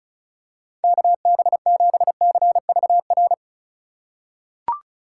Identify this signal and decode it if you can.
Morse code, heard as a keyed tone, about 36 wpm, 700 Hz. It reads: KB7CVR
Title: USA Call Signs - 6 Characters - 3x Character Spacing - 35wpm